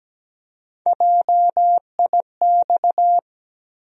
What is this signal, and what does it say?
Morse code, heard as a keyed tone, about 17 wpm, 700 Hz. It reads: JIX
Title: USA Call Sign Postfixes - 3 Letters 17wpm